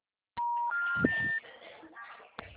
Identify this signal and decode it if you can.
Special information tone, with speech.